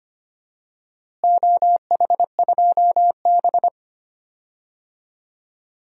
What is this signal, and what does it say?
Morse code, heard as a keyed tone, about 25 wpm, 700 Hz. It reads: OH2B
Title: DXCC-100 Call Signs - 4 Characters - Repeated 3x - 25wpm